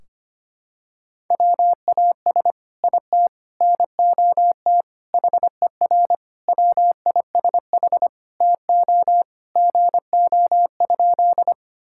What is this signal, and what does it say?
Morse code, heard as a keyed tone, about 25 wpm, 700 Hz. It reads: WAS IT NOT HER WISH TO GO?
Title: Sentences from Top 1600 Words 25wpm